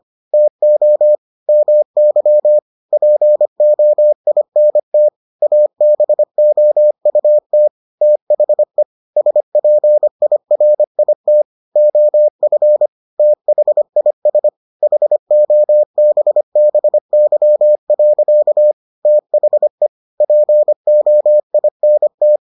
Morse code, heard as a keyed tone, about 25 wpm, 600 Hz. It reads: TO MY POINT ABOUT THE SPIRIT OF THIS HOBBY. THE POINT